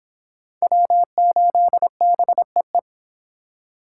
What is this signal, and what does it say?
Morse code, heard as a keyed tone, about 30 wpm, 700 Hz. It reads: W8BEE